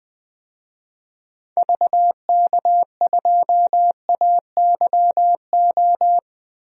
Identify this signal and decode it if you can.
Morse code, heard as a keyed tone, about 21 wpm, 700 Hz. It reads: VK2AYO
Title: DXCC-100 Call Signs - 6 Characters - Repeated 3x - 20wpm